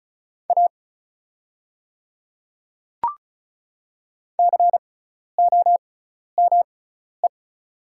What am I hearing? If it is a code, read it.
Morse code, heard as a keyed tone, about 36 wpm, spaced as for 12 wpm, 700 Hz. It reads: A COME
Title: 6x Character Spacing - Top 100 Words - 35wpm